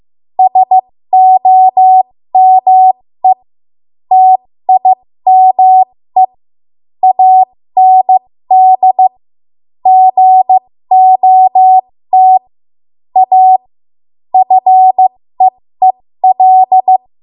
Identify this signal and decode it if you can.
Morse code, heard as a keyed tone, about 15 wpm, spaced as for 13 wpm, 750 Hz. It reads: SOME TIME AND GOT A FEEL
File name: Morse / 13WPM